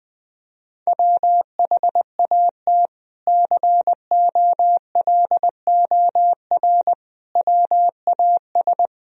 Morse code, heard as a keyed tone, about 20 wpm, 700 Hz. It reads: WHAT COLOR WAS
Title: Sentences from Top 1000 Words - Review Part 2 - 20wpm